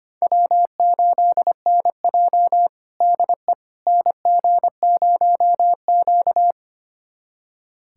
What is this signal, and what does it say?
Morse code, heard as a keyed tone, about 25 wpm, 700 Hz. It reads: W8NJ DE NG0Q
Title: USA Callsign DE Callsign - 4 Characters - 25wpm